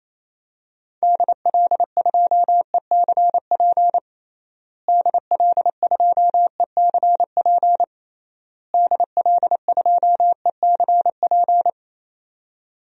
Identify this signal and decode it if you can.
Morse code, heard as a keyed tone, about 28 wpm, 700 Hz. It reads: DL2ECP DL2ECP DL2ECP
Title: DXCC-100 Call Signs - 6 Characters - Repeated 3x - 28wpm